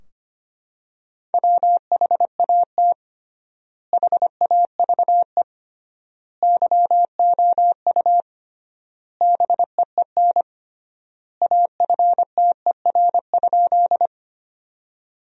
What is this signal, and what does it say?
Morse code, heard as a keyed tone, about 25 wpm, 700 Hz. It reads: WHAT HAVE YOU BEEN AFTER?
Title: 3x Word-Spacing - Sentences from Top 200 Words - 25wpm